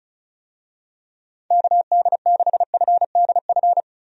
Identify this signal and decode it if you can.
Morse code, heard as a keyed tone, about 36 wpm, 700 Hz. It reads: KD6FDF
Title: USA Call Signs - 6 Characters - 35wpm